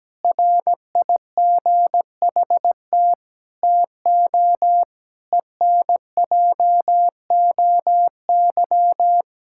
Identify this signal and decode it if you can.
Morse code, heard as a keyed tone, about 17 wpm, 700 Hz. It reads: RIGHT TO ENJOY
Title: Sentences from Top 1000 Words - Review Part 5 - 17wpm